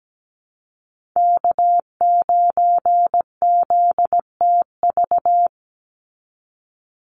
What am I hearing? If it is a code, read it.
Morse code, heard as a keyed tone, about 17 wpm, 700 Hz. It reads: K9ZTV
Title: Call Signs Repeated 2x - 17wpm